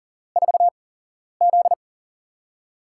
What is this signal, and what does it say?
Morse code, heard as a keyed tone, about 40 wpm, spaced as for 11 wpm, 700 Hz. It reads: 4Z
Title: USA Call Signs - 4 Characters - 8x Character Spacing - 40wpm